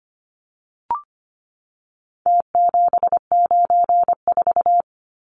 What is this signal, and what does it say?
Morse code, heard as a keyed tone, about 25 wpm, 700 Hz. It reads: T794
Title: Numbers - Contest Style - 4-Digit 25wpm